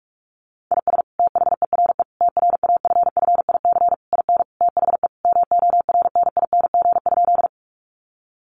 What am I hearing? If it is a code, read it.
Morse code, heard as a keyed tone, about 46 wpm, 700 Hz. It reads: IS THERE TRAFFIC IN THE MORNING?
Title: Sentences from Top 1300 Words 45wpm